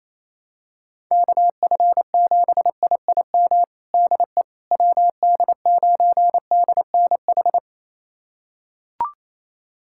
Morse code, heard as a keyed tone, about 27 wpm, 700 Hz. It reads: KF7IIM DE WD9DNH
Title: USA Callsign DE Callsign - 6 Characters - 28wpm